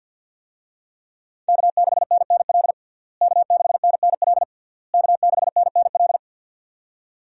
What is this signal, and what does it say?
Morse code, heard as a keyed tone, about 47 wpm, 700 Hz. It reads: K6NNL K6NNL K6NNL
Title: USA Call Signs - 5 Characters - Repeated 3x - 50wpm